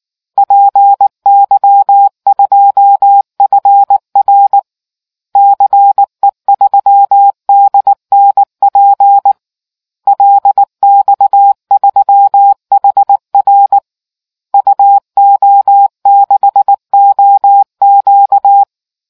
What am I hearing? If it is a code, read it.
Morse code, heard as a keyed tone, about 19 wpm, 800 Hz. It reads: PY2FR CE3DNP LX3HR UO6OQ